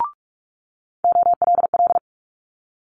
Morse code, heard as a keyed tone, about 46 wpm, 700 Hz. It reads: OLL
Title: Multi-letter Phonemes Quick Fire 45wpm